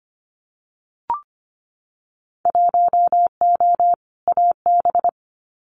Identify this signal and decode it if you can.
Morse code, heard as a keyed tone, about 25 wpm, 700 Hz. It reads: 1O AB
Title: Winter Field Day Exchanges Encore 25wpm